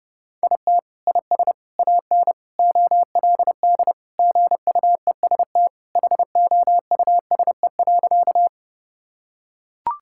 Morse code, heard as a keyed tone, about 30 wpm, 700 Hz. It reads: IT IS AN OLD GUEST HOUSE.
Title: Sentences from Top 1600 Words 30wpm